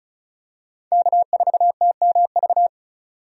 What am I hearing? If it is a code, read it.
Morse code, heard as a keyed tone, about 36 wpm, 700 Hz. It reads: K4TMV
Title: USA Call Signs - 5 Characters - 35wpm